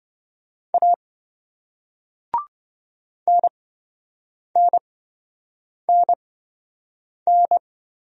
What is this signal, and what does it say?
Morse code, heard as a keyed tone, about 27 wpm, spaced as for 7 wpm, 700 Hz. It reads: A NNNN